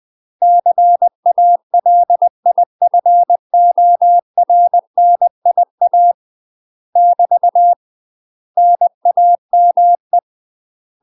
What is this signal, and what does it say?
Morse code, heard as a keyed tone, about 20 wpm, 700 Hz. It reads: CALIFORNIA <BT> NAME
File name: UK MASTER QSOs TRAINING / 20WPM / UK MASTER QSOs 0043